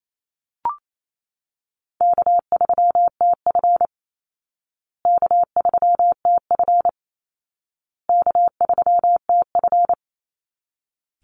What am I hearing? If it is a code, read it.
Morse code, heard as a keyed tone, about 28 wpm, 700 Hz. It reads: K3TF K3TF K3TF
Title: Easy Call Signs Repeated 3x - 28wpm